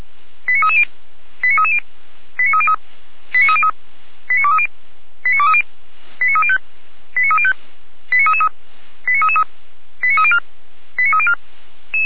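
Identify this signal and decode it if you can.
Five-tone selective call (ZVEI), heard as quick runs of five tones, tones of about 70 ms. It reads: 89330 89330 89303 89303 89230 89230 89305 89305 89333 89333 89304 89304
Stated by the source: Five-tone ZVEI selective call received by SDR, from audio.f85360.feuerwehr-orlamuende.org, a fire brigade (Feuerwehr Orlamünde).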